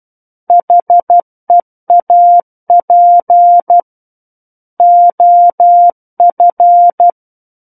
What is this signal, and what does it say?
Morse code, heard as a keyed tone, about 12 wpm, 700 Hz. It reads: HEAP OF
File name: Book One / 03-Chapter 3 - 12 WPM